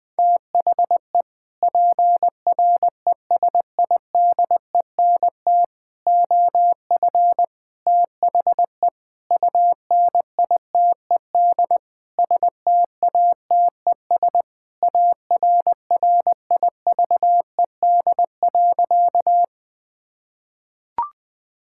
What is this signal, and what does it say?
Morse code, heard as a keyed tone, about 21 wpm, 700 Hz. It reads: THE PRESIDENT OF THE UNITED STATES ARRIVED.
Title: Sentences with 6-Word Phrases - 20wpm